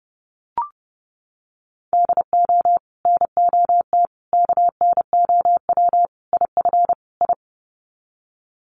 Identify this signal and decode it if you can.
Morse code, heard as a keyed tone, about 30 wpm, 700 Hz. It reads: DO NOT KNOW IF I
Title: Sets of 5 Words 30wpm